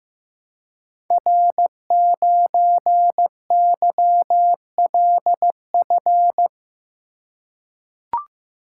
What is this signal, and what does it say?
Morse code, heard as a keyed tone, about 15 wpm, 700 Hz. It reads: R9YLF
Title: Code Groups - 5 Letter:Numbers - 15wpm